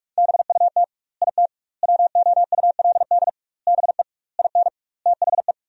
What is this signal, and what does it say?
Morse code, heard as a keyed tone, about 46 wpm, 700 Hz. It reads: BUT IT WOULD BE IN THE
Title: 2x Word-Spacing - Sentences from Top 300 Words - 45wpm